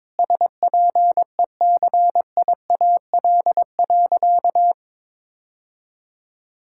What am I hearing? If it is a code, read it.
Morse code, heard as a keyed tone, about 22 wpm, 700 Hz. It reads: SPECIAL.
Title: Sentences from Top 1600 Words 22wpm